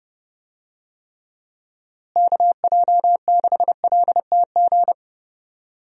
Morse code, heard as a keyed tone, about 30 wpm, 700 Hz. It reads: KJ6LTG
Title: USA Call Signs - 6 Characters - Repeated 2x - 30wpm